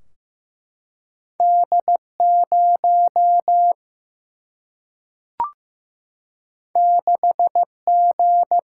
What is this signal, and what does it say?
Morse code, heard as a keyed tone, about 15 wpm, 700 Hz. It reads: D 0 6 G
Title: Single-Letter Single-Number 15wpm